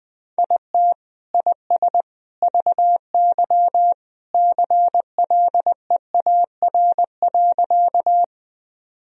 Morse code, heard as a keyed tone, about 21 wpm, 700 Hz. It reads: IT IS VY CLEAR.
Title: Sentences from Top 500 Words and Abbreviations 20wpm